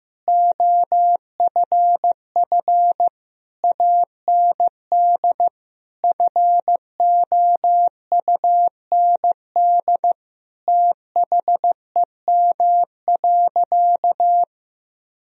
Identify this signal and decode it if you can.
Morse code, heard as a keyed tone, about 16 wpm, 700 Hz. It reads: OFF AND FOUND THEM.
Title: Sentences from Top 1000 Words - Review Part 1 - 15wpm